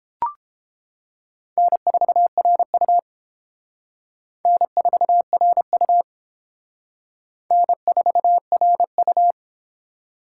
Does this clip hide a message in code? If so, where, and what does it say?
Yes: Morse code from 1.6 to 9.3 s, heard as a keyed tone, about 28 wpm, 700 Hz. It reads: N4RU N4RU N4RU